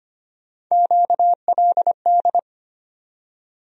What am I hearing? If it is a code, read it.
Morse code, heard as a keyed tone, about 25 wpm, 700 Hz. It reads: QLD
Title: USA Call Sign Postfixes - 3 Letters 25wpm